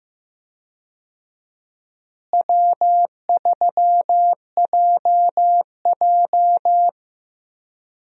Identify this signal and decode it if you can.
Morse code, heard as a keyed tone, about 15 wpm, 700 Hz. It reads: W3JJ
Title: USA Call Signs - 4 Characters - Repeated 3x - 15wpm